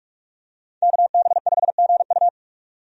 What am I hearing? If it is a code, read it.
Morse code, heard as a keyed tone, about 46 wpm, 700 Hz. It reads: KB5GU
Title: USA Call Signs - 5 Characters - 45wpm